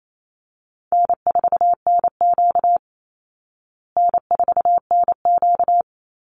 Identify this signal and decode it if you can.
Morse code, heard as a keyed tone, about 28 wpm, 700 Hz. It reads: N4NQ N4NQ
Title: Easy Call Signs Repeated 2x - 28wpm